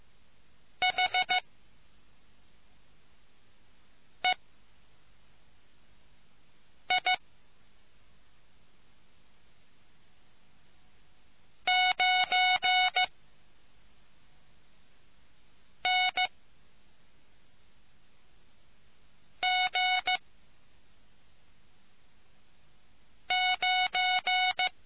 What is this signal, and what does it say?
Morse code, heard as a keyed tone, about 15 wpm, 740 Hz. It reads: H E I 9 N G 9